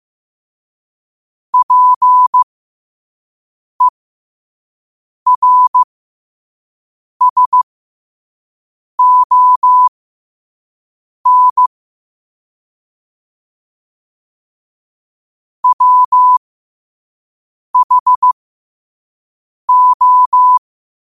Morse code, heard as a keyed tone, about 15 wpm, spaced as for 5 wpm, 1000 Hz. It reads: PERSON WHO
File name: qotdfiles / 5wpm